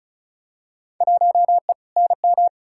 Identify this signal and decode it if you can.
Morse code, heard as a keyed tone, about 36 wpm, 700 Hz. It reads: 1E NM
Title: Field Day Exchanges - 35wpm